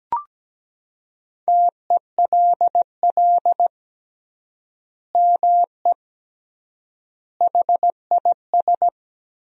Morse code, heard as a keyed tone, about 17 wpm, 700 Hz. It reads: TELL ME HIS
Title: 3x Word-Spacing - Sentences from Top 200 Words - 17wpm